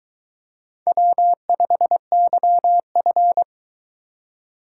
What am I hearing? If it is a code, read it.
Morse code, heard as a keyed tone, about 22 wpm, 700 Hz. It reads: W5YF